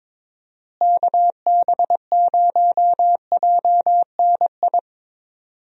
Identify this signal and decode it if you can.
Morse code, heard as a keyed tone, about 22 wpm, 700 Hz. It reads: KB0JNI
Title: USA Call Signs - 6 Characters - Repeated 2x - 22wpm